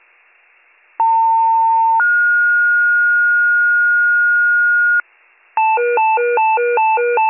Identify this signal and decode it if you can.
Two-tone page, heard as a one-second tone, then a three-second tone, about 900 then 1430 Hz, with noise. It is heard from 1.0 to 5.0 s.